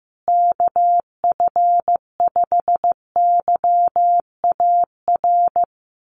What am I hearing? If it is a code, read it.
Morse code, heard as a keyed tone, about 15 wpm, 700 Hz. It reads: KF5YAR
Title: USA Call Signs Repeated 3x 15wpm